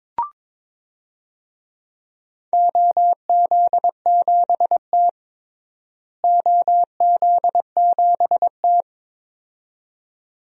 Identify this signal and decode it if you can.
Morse code, heard as a keyed tone, about 21 wpm, 700 Hz. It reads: OZ7T OZ7T
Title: DXCC-100 Call Signs - 4 Characters - Repeated 2x - 22wpm